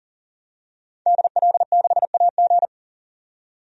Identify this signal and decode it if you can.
Morse code, heard as a keyed tone, about 40 wpm, 700 Hz. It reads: DL6AG
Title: DXCC-100 Call Signs - 5 Characters - 40wpm